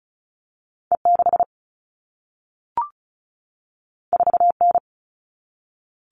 Morse code, heard as a keyed tone, about 36 wpm, 700 Hz. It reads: E6 4N
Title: All Cut Numbers - 2-Digit 35wpm